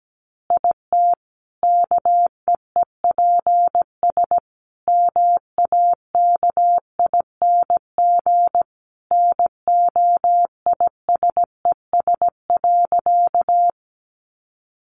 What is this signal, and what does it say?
Morse code, heard as a keyed tone, about 17 wpm, 700 Hz. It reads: IT KEEPS MAKING NOISES.
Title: Additional Easy Phrases 17wpm